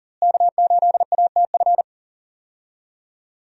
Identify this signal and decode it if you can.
Morse code, heard as a keyed tone, about 40 wpm, 700 Hz. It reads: K8ATF
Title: USA Call Signs - 5 Characters - Repeated 2x - 40wpm